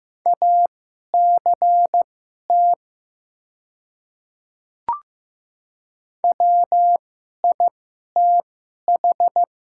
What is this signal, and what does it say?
Morse code, heard as a keyed tone, about 15 wpm, spaced as for 11 wpm, 700 Hz. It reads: ACT WITH
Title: 2x Character Spacing - Top 200 Words - 15wpm